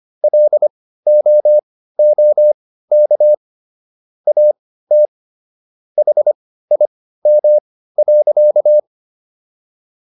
Morse code, heard as a keyed tone, about 25 wpm, spaced as for 15 wpm, 600 Hz. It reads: LOOK AT HIM.